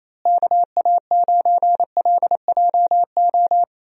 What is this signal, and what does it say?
Morse code, heard as a keyed tone, about 28 wpm, 700 Hz. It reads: KA9LJO
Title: USA Call Signs - 6 Characters - Repeated 2x - 28wpm